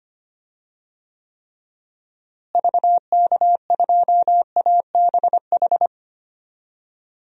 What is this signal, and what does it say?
Morse code, heard as a keyed tone, about 25 wpm, 700 Hz. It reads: VK2ABH
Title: DXCC-100 Call Signs - 6 Characters - Repeated 2x - 25wpm